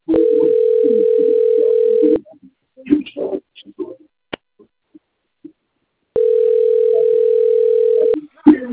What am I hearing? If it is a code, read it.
Ringback tone (North America), 440 + 480 Hz, with speech.
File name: out- - - - - .63471